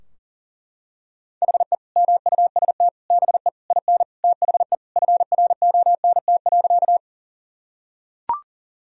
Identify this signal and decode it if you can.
Morse code, heard as a keyed tone, about 40 wpm, 700 Hz. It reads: HE MUST BE IN THE FRONT.
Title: Sentences from Top 1000 Words - Review Part 2 - 40wpm